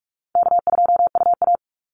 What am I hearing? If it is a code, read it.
Morse code, heard as a keyed tone, about 46 wpm, 700 Hz. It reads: K2UA
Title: Easy Call Signs Repeated 3x - 45wpm